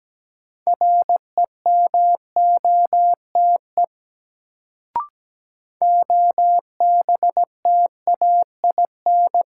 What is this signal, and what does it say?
Morse code, heard as a keyed tone, about 17 wpm, 700 Hz. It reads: REMOTE OBTAIN
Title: Sentences from Top 1300 Words - New Word Drill Alt - 17wpm